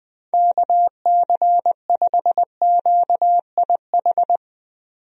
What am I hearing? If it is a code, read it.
Morse code, heard as a keyed tone, about 20 wpm, 700 Hz. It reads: KC5QIH
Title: USA Call Signs - 6 Characters - 20wpm